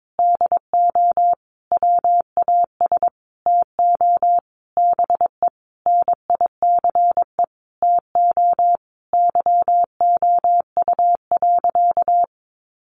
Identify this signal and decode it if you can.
Morse code, heard as a keyed tone, about 22 wpm, 700 Hz. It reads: DO WAS TO BE NICE TO YOU.
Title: Sentences from Top 1100 Words 22wpm